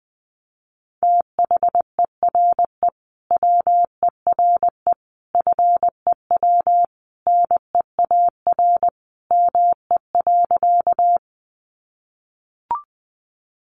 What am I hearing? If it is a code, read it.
Morse code, heard as a keyed tone, about 20 wpm, 700 Hz. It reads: THERE WERE FEW NEAR ME.
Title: Sentences from Top 300 Words 20wpm